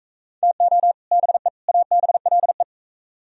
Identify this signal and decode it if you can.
Morse code, heard as a keyed tone, about 42 wpm, 700 Hz. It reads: TO BE ABLE